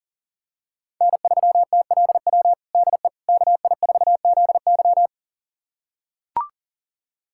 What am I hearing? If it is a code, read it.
Morse code, heard as a keyed tone, about 40 wpm, 700 Hz. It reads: N3TLW DE KI4ZY
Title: USA Callsign DE Callsign - 5 Characters - 40wpm